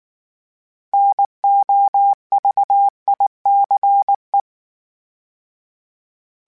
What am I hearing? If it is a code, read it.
Morse code, heard as a keyed tone, about 19 wpm, 800 Hz. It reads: NOVICE